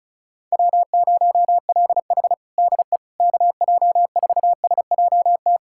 Morse code, heard as a keyed tone, about 36 wpm, 700 Hz. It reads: W0LH DE KJ4SJT
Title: USA Callsign DE Callsign 35wpm